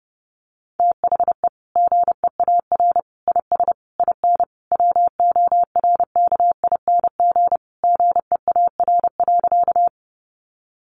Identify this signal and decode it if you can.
Morse code, heard as a keyed tone, about 29 wpm, 700 Hz. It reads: THE GEAR IS IN WORKING GEAR.
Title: Sentences from Top 1400 Words 30wpm